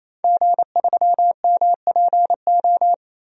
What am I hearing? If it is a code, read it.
Morse code, heard as a keyed tone, about 27 wpm, 700 Hz. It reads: G3MPO
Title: DXCC-100 Call Signs - 5 Characters - 28wpm